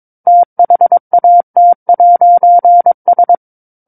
Morse code, heard as a keyed tone, about 22 wpm, 700 Hz. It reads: THAT'S
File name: 06-Chapter-06 22 WPM